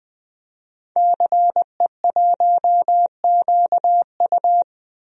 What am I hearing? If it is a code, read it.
Morse code, heard as a keyed tone, about 20 wpm, 700 Hz. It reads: CE1QU
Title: DXCC-100 Call Signs - 5 Characters - 20wpm